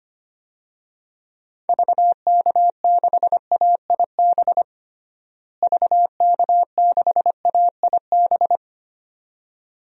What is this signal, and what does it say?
Morse code, heard as a keyed tone, about 25 wpm, 700 Hz. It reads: VK6AIB VK6AIB
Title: DXCC-100 Call Signs - 6 Characters - Repeated 2x - 25wpm